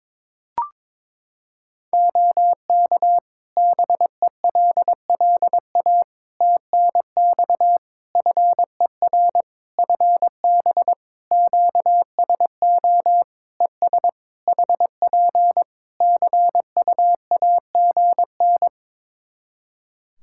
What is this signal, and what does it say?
Morse code, heard as a keyed tone, about 22 wpm, 700 Hz. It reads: OK BELLA TNX FER FB QSO ES HP CUAGN
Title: K1USN SST QSO Practice 22wpm